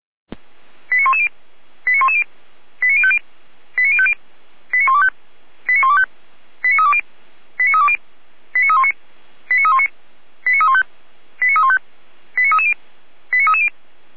Five-tone selective call (ZVEI), heard as quick runs of five tones, tones of about 70 ms. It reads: 89110 89110 89050 89050 89125 89125 89320 89320 89319 89319 89315 89315 89330 89330